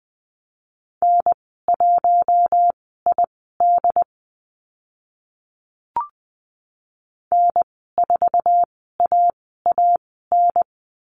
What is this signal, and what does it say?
Morse code, heard as a keyed tone, about 20 wpm, spaced as for 15 wpm, 700 Hz. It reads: N1ID N4AAN
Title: 069 Lesson 12g - Letter d - Call Signs 20wpm